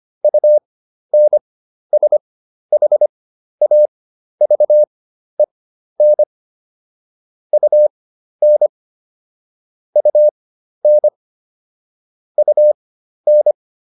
Morse code, heard as a keyed tone, about 25 wpm, spaced as for 12 wpm, 600 Hz. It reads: UNSHAVEN UN UN UN